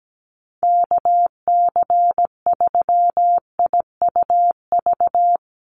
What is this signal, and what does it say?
Morse code, heard as a keyed tone, about 17 wpm, 700 Hz. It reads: KC3IUV
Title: USA Call Signs Repeated 2x 17wpm